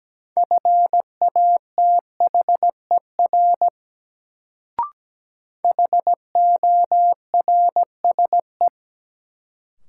Morse code, heard as a keyed tone, about 17 wpm, 700 Hz. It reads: FATHER HORSE
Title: Sentences from Top 300 Words - New Word Drill - 17wpm